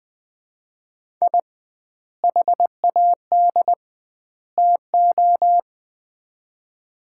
Morse code, heard as a keyed tone, about 21 wpm, 700 Hz. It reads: I HAD TO
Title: 2x Word-Spacing - Sets of 3 Words - 20wpm